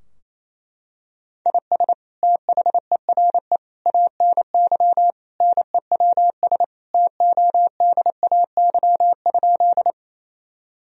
Morse code, heard as a keyed tone, about 28 wpm, 700 Hz. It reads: IS THERE ANY NEWS TODAY?
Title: Sentences from Top 1000 Words - Review Part 4 - 28wpm